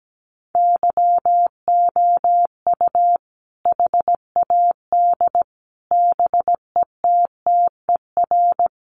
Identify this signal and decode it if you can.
Morse code, heard as a keyed tone, about 17 wpm, 700 Hz. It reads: YOU HAD BETTER
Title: Sentences from Top 700 Words 17wpm